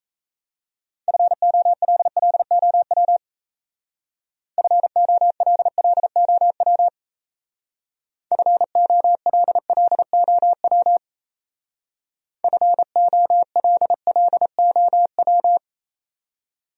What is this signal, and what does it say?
Morse code, heard as a keyed tone, about 36 wpm, 700 Hz. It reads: FOLLOW FOLLOW FOLLOW FOLLOW